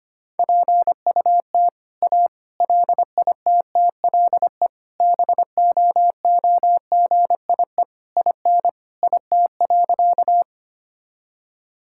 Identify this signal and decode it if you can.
Morse code, heard as a keyed tone, about 25 wpm, 700 Hz. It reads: PUT A LITTLE BOOGIE IN IT.
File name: practice / 25wpm / All-Age Appropriate Jokes - 25wpm